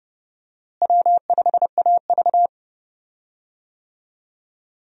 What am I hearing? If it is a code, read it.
Morse code, heard as a keyed tone, about 30 wpm, 700 Hz. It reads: W5AV
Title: USA Call Signs - 4 Characters - Repeated 3x - 30wpm